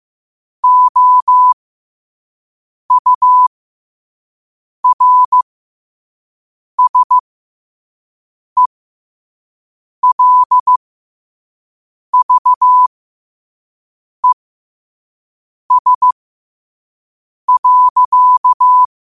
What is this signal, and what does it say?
Morse code, heard as a keyed tone, about 15 wpm, spaced as for 5 wpm, 1000 Hz. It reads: OURSELVES.
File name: qotdfiles / 5wpm